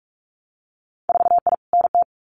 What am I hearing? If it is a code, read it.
Morse code, heard as a keyed tone, about 46 wpm, 700 Hz. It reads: 4I NT
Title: Winter Field Day Exchanges Encore 45wpm